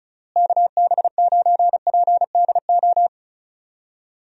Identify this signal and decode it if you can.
Morse code, heard as a keyed tone, about 34 wpm, 700 Hz. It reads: KB9PDO
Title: USA Call Signs - 6 Characters - Repeated 3x - 35wpm